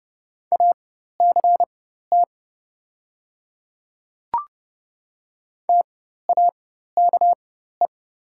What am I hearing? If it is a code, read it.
Morse code, heard as a keyed tone, about 30 wpm, spaced as for 14 wpm, 700 Hz. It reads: ACT TAKE
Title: 4x Character Spacing - Top 200 Words - 30wpm